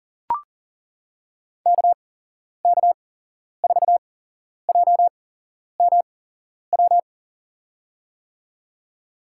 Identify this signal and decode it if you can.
Morse code, heard as a keyed tone, about 40 wpm, spaced as for 11 wpm, 700 Hz. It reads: KK4JMW
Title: USA Call Signs - 6 Characters - 8x Character Spacing - 40wpm